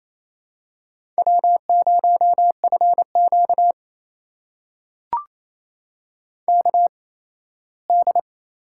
Morse code, heard as a keyed tone, about 28 wpm, spaced as for 17 wpm, 700 Hz. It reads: W0FQ KD
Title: USA Call Signs - 8x Character Spacing - 28wpm